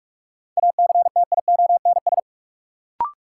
Morse code, heard as a keyed tone, about 45 wpm, 700 Hz. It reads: ACTIONS
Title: Sentences from Top 1000 Words - Review Part 6 - New Word Drill - 45wpm